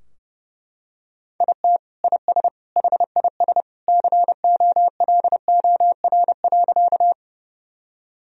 Morse code, heard as a keyed tone, about 30 wpm, 700 Hz. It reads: IT IS HIS COLOR.
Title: Sentences from Top 1000 Words - Review Part 2 - 30wpm